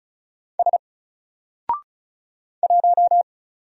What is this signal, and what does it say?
Morse code, heard as a keyed tone, about 36 wpm, 700 Hz. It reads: S 1
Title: Single Character (Letter:Number:Punctuation:Prosign) Alt 35wpm